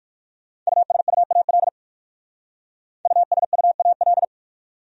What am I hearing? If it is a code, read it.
Morse code, heard as a keyed tone, about 56 wpm, 700 Hz. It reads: USUAL USUAL
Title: Speed-Racing Top 500 Words 45wpm